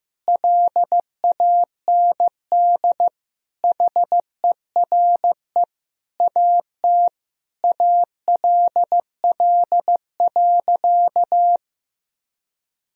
Morse code, heard as a keyed tone, about 16 wpm, 700 Hz. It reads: LAND HERE AT ALL.
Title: Sentences from Top 1000 Words - Review Part 1 - 15wpm